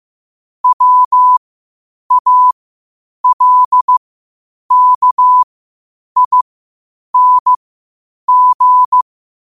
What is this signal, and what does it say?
Morse code, heard as a keyed tone, about 15 wpm, spaced as for 8 wpm, 1000 Hz. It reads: WALKING
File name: qotdfiles / 8wpm